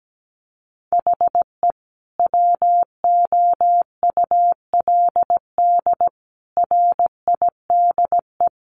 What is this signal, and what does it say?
Morse code, heard as a keyed tone, about 17 wpm, 700 Hz. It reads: HE WOULD RIDE
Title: Sentences from Top 1200 Words 17wpm